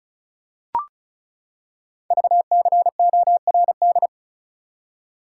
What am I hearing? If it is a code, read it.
Morse code, heard as a keyed tone, about 34 wpm, 700 Hz. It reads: VCORD
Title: Code Groups - 5 Letters - 35wpm